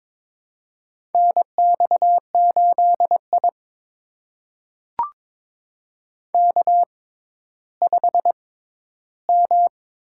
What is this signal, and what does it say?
Morse code, heard as a keyed tone, about 22 wpm, spaced as for 12 wpm, 700 Hz. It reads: NX8I K5M
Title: USA Call Signs - 4 Characters - 6x Character Spacing - 22wpm